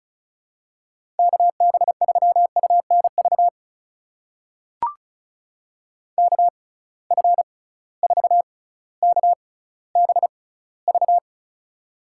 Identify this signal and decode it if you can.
Morse code, heard as a keyed tone, about 34 wpm, spaced as for 18 wpm, 700 Hz. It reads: KB3UNV KF4KBV
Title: USA Call Signs - 6 Characters - 6x Character Spacing - 35wpm